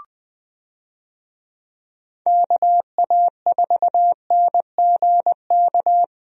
Morse code, heard as a keyed tone, about 20 wpm, 700 Hz. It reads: KA4NGK
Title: USA Call Signs - 6 Characters - Repeated 3x - 20wpm